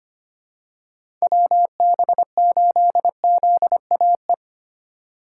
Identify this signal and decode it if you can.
Morse code, heard as a keyed tone, about 25 wpm, 700 Hz. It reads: WB8ZAE
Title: USA Call Signs - 6 Characters - 4x Character Spacing - 25wpm